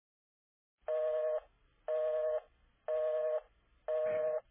Busy tone (North America), 480 + 620 Hz.